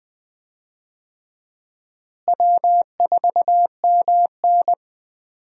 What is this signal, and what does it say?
Morse code, heard as a keyed tone, about 20 wpm, 700 Hz. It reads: W4MN
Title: USA Call Signs - 4 Characters - Repeated 3x - 20wpm